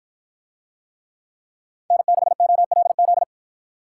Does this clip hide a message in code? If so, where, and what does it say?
Yes: Morse code from 1.9 to 3.2 s, heard as a keyed tone, about 53 wpm, 700 Hz. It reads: N6OLZ